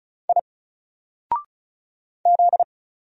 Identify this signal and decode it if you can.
Morse code, heard as a keyed tone, about 35 wpm, 700 Hz. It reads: I Z
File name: Single-Letter-Number-35wpm